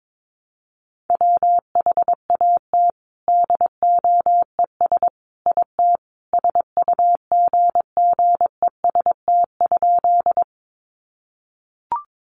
Morse code, heard as a keyed tone, about 22 wpm, 700 Hz. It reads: WHAT DOES IT SUGGEST?
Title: Sentences from Top 1300 Words 22wpm